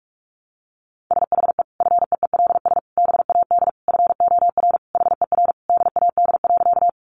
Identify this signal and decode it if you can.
Morse code, heard as a keyed tone, about 44 wpm, 700 Hz. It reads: SHE FEELS BAD FOR HER DAD.
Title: Additional Easy Phrases 45wpm